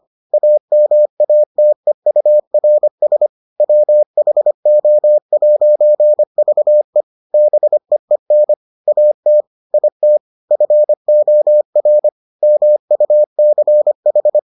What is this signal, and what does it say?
Morse code, heard as a keyed tone, about 25 wpm, 600 Hz. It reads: AMATEURS WHO'VE BEEN AT IT FOR MUCH